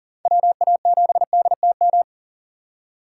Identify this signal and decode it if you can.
Morse code, heard as a keyed tone, about 40 wpm, 700 Hz. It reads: WA7DTM
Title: USA Call Signs - 6 Characters - 2x Character Spacing - 40wpm